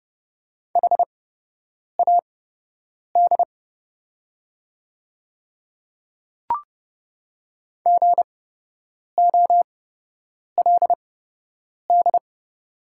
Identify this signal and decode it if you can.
Morse code, heard as a keyed tone, about 30 wpm, spaced as for 8 wpm, 700 Hz. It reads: HAD GOLD
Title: 8x Character Spacing - Top 500 Words - 30wpm